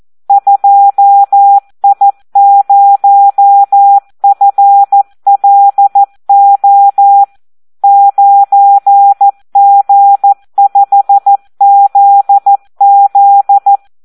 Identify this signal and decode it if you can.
Morse code, heard as a keyed tone, about 14 wpm, 800 Hz. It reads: 2I0FLO 9G5ZZ
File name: Audio practica CW / SD 1 14